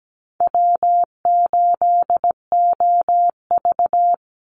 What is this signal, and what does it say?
Morse code, heard as a keyed tone, about 17 wpm, 700 Hz. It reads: W8OV
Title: Easy Call Signs - 17wpm